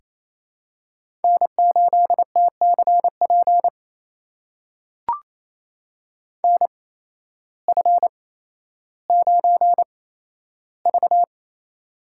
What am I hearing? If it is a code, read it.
Morse code, heard as a keyed tone, about 28 wpm, spaced as for 13 wpm, 700 Hz. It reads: N8TCP NF9V
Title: USA Call Signs - 8x Character Spacing - 28wpm